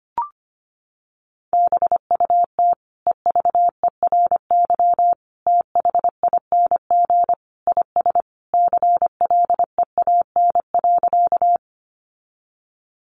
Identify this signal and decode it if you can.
Morse code, heard as a keyed tone, about 25 wpm, 700 Hz. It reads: BUT EVERYTHING IS CLEAN.
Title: Sentences from Top 1100 Words 25wpm